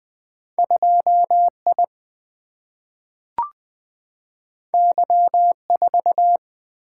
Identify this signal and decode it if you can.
Morse code, heard as a keyed tone, about 20 wpm, 700 Hz. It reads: 2 I Y 4
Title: Single-Letter Single-Number 20wpm